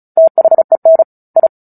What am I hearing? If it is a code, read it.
Morse code, heard as a keyed tone, about 34 wpm, 650 Hz. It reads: THEN I
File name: Book Two / 09-Chapter 9 - 35 WPM